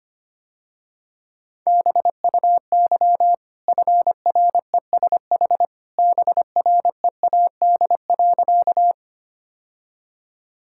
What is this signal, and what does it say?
Morse code, heard as a keyed tone, about 25 wpm, 700 Hz. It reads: BUY FRESH BREAD.
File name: Easy Sentences with Hints - 25wpm